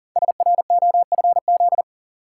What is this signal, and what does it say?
Morse code, heard as a keyed tone, about 42 wpm, 700 Hz. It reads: SROFZ
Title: Code Groups - 5 Letters - 40wpm